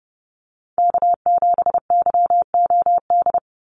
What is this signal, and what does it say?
Morse code, heard as a keyed tone, about 30 wpm, 700 Hz. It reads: K7YOD
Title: USA Call Signs 30wpm